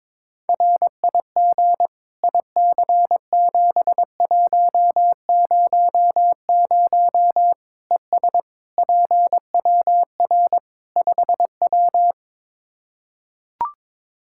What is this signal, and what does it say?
Morse code, heard as a keyed tone, about 21 wpm, 700 Hz. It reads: RIG IC7100 ES PWR 5W
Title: Practice QSO - Exchange 2 - Info - 22wpm